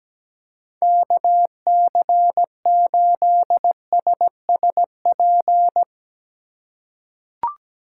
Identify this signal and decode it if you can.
Morse code, heard as a keyed tone, about 17 wpm, 700 Hz. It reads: KC8SSP
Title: USA Call Signs - 3x Character Spacing - 17wpm